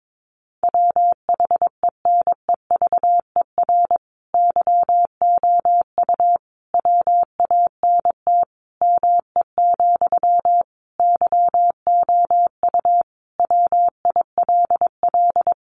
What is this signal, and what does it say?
Morse code, heard as a keyed tone, about 22 wpm, 700 Hz. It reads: WHENEVER YOU WANT ME, YOU WILL
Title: Sentences from Top 1200 Words 22wpm